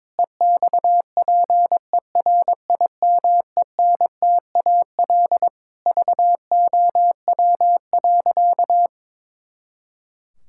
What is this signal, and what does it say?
Morse code, heard as a keyed tone, about 22 wpm, 700 Hz. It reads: EXPERIMENTAL VOW.
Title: Pangram Challenge v2 - 22wpm